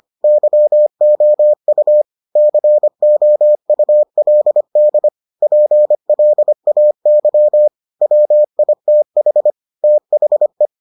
Morse code, heard as a keyed tone, about 25 wpm, 600 Hz. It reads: YOU COULD PLAY WITH THE